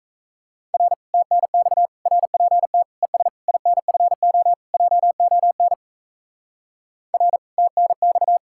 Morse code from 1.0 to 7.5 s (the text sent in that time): TNX RPT ES INFO JON R